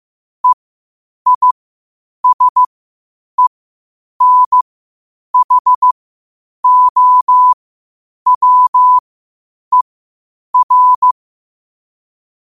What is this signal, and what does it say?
Morse code, heard as a keyed tone, about 15 wpm, spaced as for 8 wpm, 1000 Hz. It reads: EISENHOWER
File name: qotdfiles / 8wpm